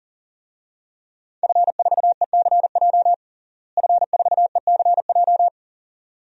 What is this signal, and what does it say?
Morse code, heard as a keyed tone, about 40 wpm, 700 Hz. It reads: F4ECJ F4ECJ
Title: DXCC-100 Call Signs - 5 Characters - Repeated 2x - 40wpm